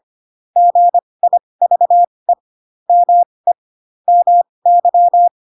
Morse code, heard as a keyed tone, about 25 wpm, spaced as for 20 wpm, 700 Hz. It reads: GIVE ME MY
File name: Pinocchio / Pinocchio ch3 15 25 ik1hge